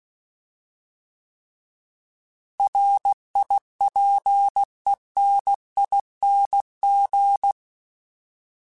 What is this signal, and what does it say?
Morse code, heard as a keyed tone, about 16 wpm, 800 Hz. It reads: RIPENING
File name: Alphabetical / Alphabetical mp3s / Rs @15